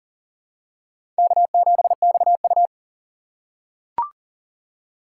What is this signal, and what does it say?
Morse code, heard as a keyed tone, about 40 wpm, 700 Hz. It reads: K7XU
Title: USA Call Signs - 6x Character Spacing - 40wpm